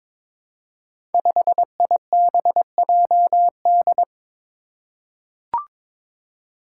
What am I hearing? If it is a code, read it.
Morse code, heard as a keyed tone, about 23 wpm, 700 Hz. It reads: 5IBJD
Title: Five Challenging Letter-Number Combinations 22wpm